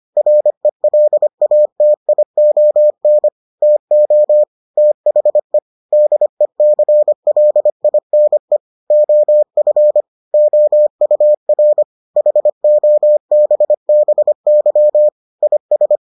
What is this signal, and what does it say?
Morse code, heard as a keyed tone, about 25 wpm, 600 Hz. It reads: RELATION TO THE DECLINE OF OUR HOBBY IS